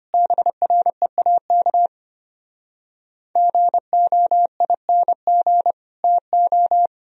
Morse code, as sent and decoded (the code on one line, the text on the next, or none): -... .-. .- -.- / --. --- .. -. --. / - ---
BREAK GOING TO